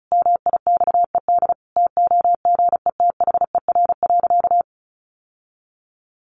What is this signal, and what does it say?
Morse code, heard as a keyed tone, about 34 wpm, 700 Hz. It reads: MIXED TOGETHER.
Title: Sentences from Top 1500 Words 35wpm